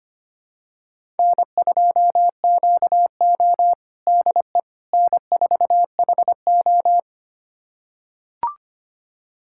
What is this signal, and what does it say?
Morse code, heard as a keyed tone, about 25 wpm, 700 Hz. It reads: N2QO DE N4HO
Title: USA Callsign DE Callsign - 4 Characters - 25wpm